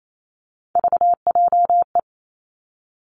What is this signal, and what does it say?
Morse code, heard as a keyed tone, about 28 wpm, 700 Hz. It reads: VJE
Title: Three Challenging Letter-Number Combinations 28wpm